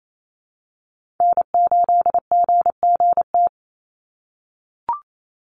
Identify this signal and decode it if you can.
Morse code, heard as a keyed tone, about 28 wpm, 700 Hz. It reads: N8GGT
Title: USA Call Signs 28wpm